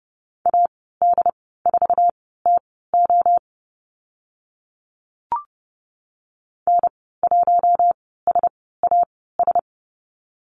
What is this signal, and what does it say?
Morse code, heard as a keyed tone, about 30 wpm, spaced as for 17 wpm, 700 Hz. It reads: AD4TO N1SAS
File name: Lesson 15g - Number 5 - Call Signs 30wpm